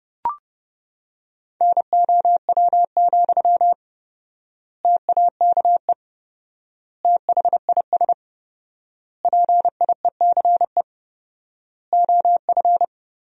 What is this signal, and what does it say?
Morse code, heard as a keyed tone, about 30 wpm, 700 Hz. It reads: NOW, TAKE THIS PIECE OF
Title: 4x Word-Spacing - Sentences from Top 400 Words - 30wpm